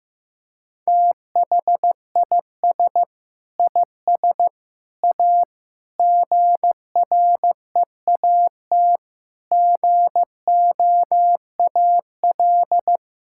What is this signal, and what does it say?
Morse code, heard as a keyed tone, about 16 wpm, 700 Hz. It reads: THIS IS A GREAT GOAL
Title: Sentences from Top 1000 Words - Review Part 6 - 15wpm